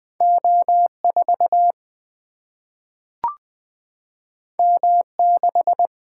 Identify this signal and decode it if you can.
Morse code, heard as a keyed tone, about 20 wpm, 700 Hz. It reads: O4 M6
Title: Single-Letter Single-Number 20wpm